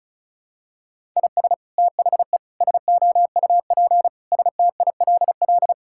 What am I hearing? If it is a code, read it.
Morse code, heard as a keyed tone, about 36 wpm, 700 Hz. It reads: IS THE SOUP STILL
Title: Sentences from Top 1000 Words - Review Part 6 - 35wpm